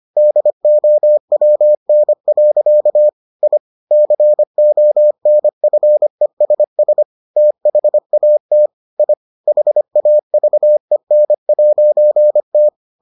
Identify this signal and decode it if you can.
Morse code, heard as a keyed tone, about 25 wpm, 600 Hz. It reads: DOWN. I CONFESS THAT I HAVEN'T